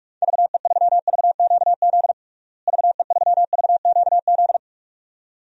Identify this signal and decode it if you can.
Morse code, heard as a keyed tone, about 46 wpm, 700 Hz. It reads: VE3VQZ VE3VQZ
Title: DXCC-100 Call Signs - 6 Characters - Repeated 2x - 45wpm